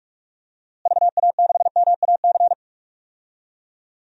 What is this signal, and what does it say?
Morse code, heard as a keyed tone, about 43 wpm, 700 Hz. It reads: VA6MAC
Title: DXCC-100 Call Signs - 6 Characters - Repeated 3x - 45wpm